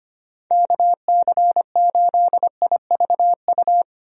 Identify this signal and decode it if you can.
Morse code, heard as a keyed tone, about 25 wpm, 700 Hz. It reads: KC8IVU
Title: USA Call Signs - 6 Characters - Repeated 3x - 25wpm